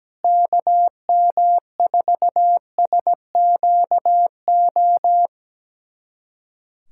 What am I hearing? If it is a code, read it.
Morse code, heard as a keyed tone, about 17 wpm, 700 Hz. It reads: KM4SQO
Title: USA Callsign DE Callsign 17wpm